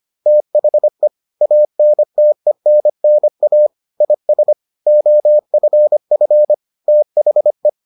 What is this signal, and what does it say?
Morse code, heard as a keyed tone, about 25 wpm, 600 Hz. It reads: THE ANTENNA IS OFF THE